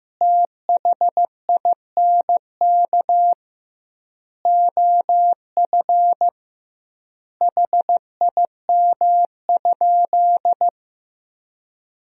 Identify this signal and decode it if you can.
Morse code, heard as a keyed tone, about 16 wpm, 700 Hz. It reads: THINK OF HIM?
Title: 2x Word-Spacing - Sentences from Top 200 Words - 15wpm